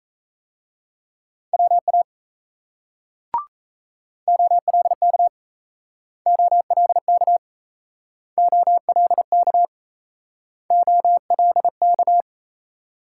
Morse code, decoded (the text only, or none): WA OLK OLK OLK OLK